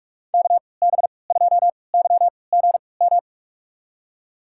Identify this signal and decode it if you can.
Morse code, heard as a keyed tone, about 44 wpm, spaced as for 25 wpm, 700 Hz. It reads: KB2YGM
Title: USA Call Signs - 3x Character Spacing - 45wpm